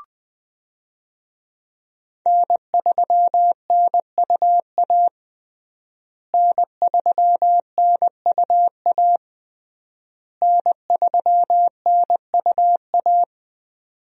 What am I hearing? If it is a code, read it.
Morse code, heard as a keyed tone, about 20 wpm, 700 Hz. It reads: N3NUA N3NUA N3NUA
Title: USA Call Signs - 5 Characters - Repeated 3x - 20wpm